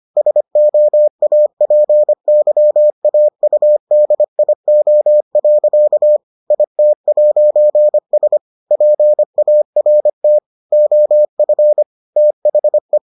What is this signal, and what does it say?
Morse code, heard as a keyed tone, about 25 wpm, 600 Hz. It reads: SOAPYAUDIO. IT'S PART OF THE